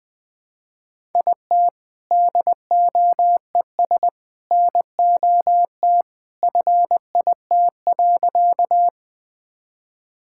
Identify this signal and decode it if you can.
Morse code, heard as a keyed tone, about 20 wpm, 700 Hz. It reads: IT DOES NOT FIT.
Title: Sentences with 2-Word Phrases - 20wpm